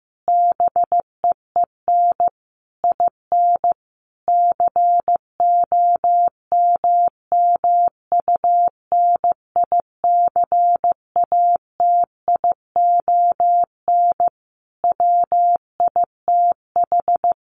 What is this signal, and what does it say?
Morse code, heard as a keyed tone, about 15 wpm, 700 Hz. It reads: BEEN IN COMMUNICATION WITH
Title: Sentences from Top 1400 Words 15wpm